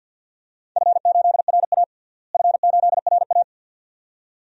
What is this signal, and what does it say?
Morse code, heard as a keyed tone, about 52 wpm, 700 Hz. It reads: F8RA F8RA
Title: DXCC-100 Call Signs - 4 Characters - Repeated 2x - 50wpm